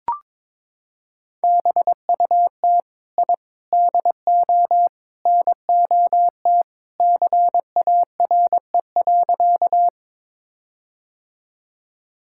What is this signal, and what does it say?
Morse code, heard as a keyed tone, about 22 wpm, 700 Hz. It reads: BUT I DO NOT CARE.
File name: practice / 22wpm / Sentences with 4-Word Phrases - 22wpm